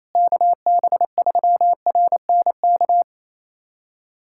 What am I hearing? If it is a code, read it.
Morse code, heard as a keyed tone, about 28 wpm, 700 Hz. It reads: KB3RNK
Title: USA Callsign DE Callsign 28wpm